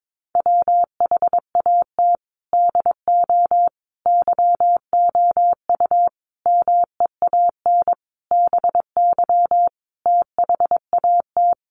Morse code, heard as a keyed tone, about 22 wpm, 700 Hz. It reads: WHAT DO YOU MEAN BY THAT
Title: Sentences from Top 1200 Words 22wpm